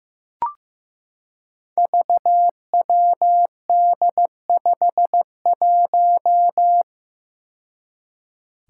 Morse code, heard as a keyed tone, about 15 wpm, 700 Hz. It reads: VWD51
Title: Five Challenging Letter-Number Combinations 15wpm